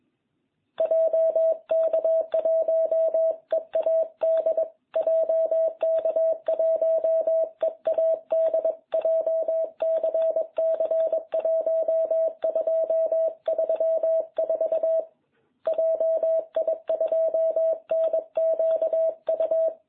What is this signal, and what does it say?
Morse code, heard as a keyed tone, about 21 wpm, 660 Hz. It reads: JK1EAD JK1EAD JCC 1234 JI2NQU